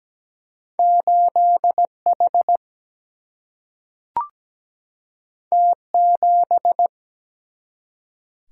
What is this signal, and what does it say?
Morse code, heard as a keyed tone, about 17 wpm, 700 Hz. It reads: 8H T7
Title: Single-Letter Single-Number 17wpm